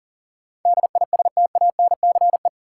Morse code, heard as a keyed tone, about 40 wpm, 700 Hz. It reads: DISTANCE
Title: Sentences from Top 1200 Words - New Word Drill Alt - 40wpm